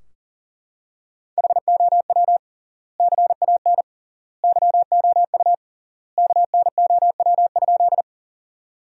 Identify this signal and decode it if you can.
Morse code, heard as a keyed tone, about 40 wpm, 700 Hz. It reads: HOW CAN YOU KNOW?
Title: 3x Word-Spacing - Sentences from Top 100 Words - 40wpm